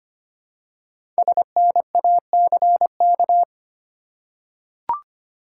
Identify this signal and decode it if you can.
Morse code, heard as a keyed tone, about 25 wpm, 700 Hz. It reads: SNACK
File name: practice / 25wpm / Easy Words with Hints - 25wpm